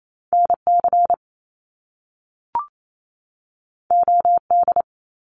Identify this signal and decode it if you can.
Morse code, heard as a keyed tone, about 28 wpm, 700 Hz. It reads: NC OD
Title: Top Two Letter Combinations in Words 28wpm